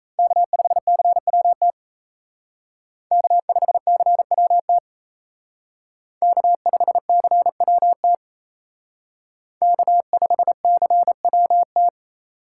Morse code, heard as a keyed tone, about 36 wpm, 700 Hz. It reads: K5CWT K5CWT K5CWT K5CWT